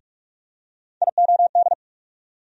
Morse code, heard as a keyed tone, about 45 wpm, 700 Hz. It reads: IOD
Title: USA Call Sign Postfixes - 3 Letters 45wpm